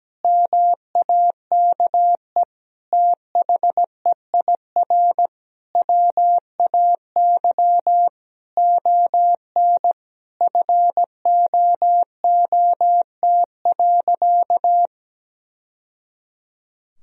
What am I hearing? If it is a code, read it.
Morse code, heard as a keyed tone, about 17 wpm, 700 Hz. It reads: MAKE THEIR WAY ON FOOT.
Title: Sentences from Top 1000 Words - Review Part 3 - 17wpm